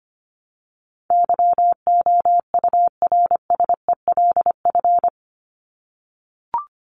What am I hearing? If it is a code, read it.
Morse code, heard as a keyed tone, about 25 wpm, 700 Hz. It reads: YOURSELF
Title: Top 501-750 Words 25wpm